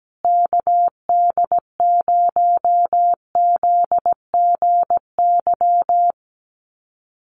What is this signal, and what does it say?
Morse code, heard as a keyed tone, about 17 wpm, 700 Hz. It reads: KD0ZGY
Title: USA Call Signs Repeated 3x 17wpm